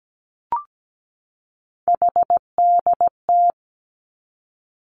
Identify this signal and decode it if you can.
Morse code, heard as a keyed tone, about 17 wpm, 700 Hz. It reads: HDT
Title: Three Challenging Letter-Number Combinations 17wpm